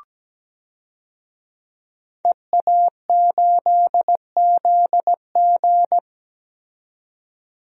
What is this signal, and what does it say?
Morse code, heard as a keyed tone, about 17 wpm, 700 Hz. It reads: EA8ZG
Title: DXCC-100 Call Signs - 5 Characters - 17wpm